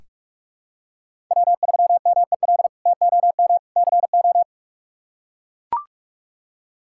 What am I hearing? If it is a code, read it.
Morse code, heard as a keyed tone, about 41 wpm, 700 Hz. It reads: W3MEL TOM CO
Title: K1USN SST QSO Practice - 45wpm